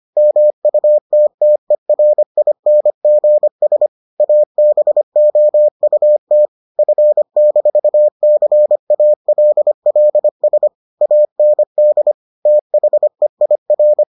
Morse code, heard as a keyed tone, about 25 wpm, 600 Hz. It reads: MUTTERINGS ABOUT F-CALLS AND THEIR